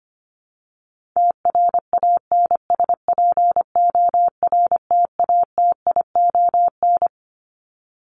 Morse code, heard as a keyed tone, about 25 wpm, 700 Hz. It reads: TRANSPORTATION
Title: Top 1501-1750 Words 25wpm